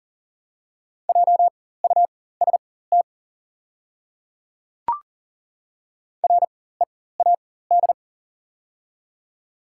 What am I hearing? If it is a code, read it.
Morse code, heard as a keyed tone, about 40 wpm, spaced as for 19 wpm, 700 Hz. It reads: JUST READ
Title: 4x Character Spacing - Top 300 Words - 40wpm